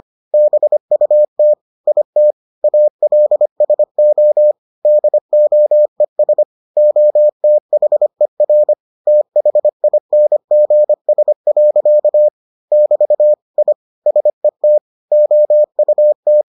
Morse code, heard as a keyed tone, about 25 wpm, 600 Hz. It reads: BUT IT ALSO DOES OTHER THINGS. <BT> I SET OUT